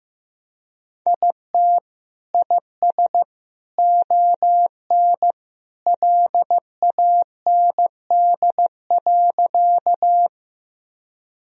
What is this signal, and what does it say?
Morse code, heard as a keyed tone, about 15 wpm, 700 Hz. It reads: IT IS ON LAND.
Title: Sentences from Top 1000 Words - Review Part 1 - 15wpm